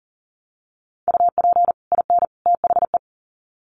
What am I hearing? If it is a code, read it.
Morse code, heard as a keyed tone, about 40 wpm, 700 Hz. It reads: UP IN THE
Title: Sets of 3 Words 40wpm